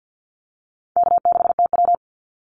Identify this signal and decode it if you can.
Morse code, heard as a keyed tone, about 50 wpm, 700 Hz. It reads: K6TW
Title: NA SOTA Activator Callsigns 50wpm